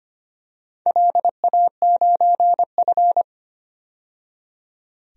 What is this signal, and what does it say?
Morse code, heard as a keyed tone, about 25 wpm, 700 Hz. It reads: LA9F
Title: DXCC-100 Call Signs - 4 Characters - 25wpm